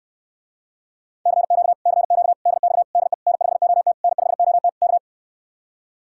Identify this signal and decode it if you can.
Morse code, heard as a keyed tone, about 69 wpm, 700 Hz. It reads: CQ CQ DX DE N5OT N5OT K